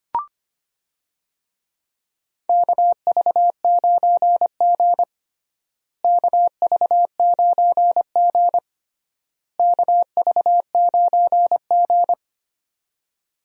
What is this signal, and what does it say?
Morse code, heard as a keyed tone, about 25 wpm, 700 Hz. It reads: KV9G KV9G KV9G
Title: USA Call Signs - 4 Characters - Repeated 3x - 25wpm